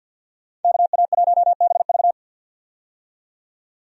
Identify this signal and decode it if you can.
Morse code, heard as a keyed tone, about 50 wpm, 700 Hz. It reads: KA1BV
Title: USA Call Signs - 5 Characters - 50wpm